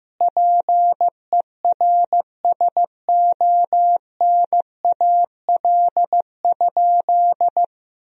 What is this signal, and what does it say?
Morse code, heard as a keyed tone, about 15 wpm, 700 Hz. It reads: PERSONAL?
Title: Sentences from Top 1000 Words - Review Part 4 - 15wpm